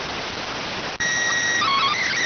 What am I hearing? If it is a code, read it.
SSTV (slow-scan television), PD 290.